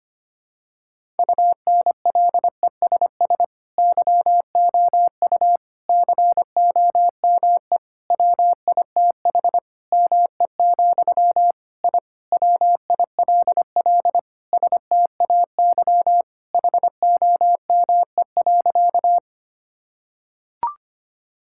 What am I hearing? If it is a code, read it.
Morse code, heard as a keyed tone, about 25 wpm, 700 Hz. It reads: UNLESS YOU COME WITH ME, I WILL STAY HOME.
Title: Sentences from Top 1000 Words - Review Part 6 - 25wpm